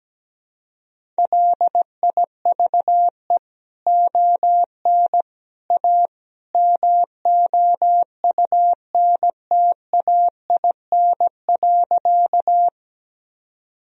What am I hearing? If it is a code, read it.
Morse code, heard as a keyed tone, about 17 wpm, 700 Hz. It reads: LIVE ON A MOUNTAIN.
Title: Sentences from Top 1000 Words - Review Part 1 - 17wpm